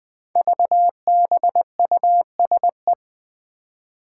Morse code, heard as a keyed tone, about 20 wpm, 700 Hz. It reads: VBUSE
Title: Five Challenging Letter-Number Combinations 20wpm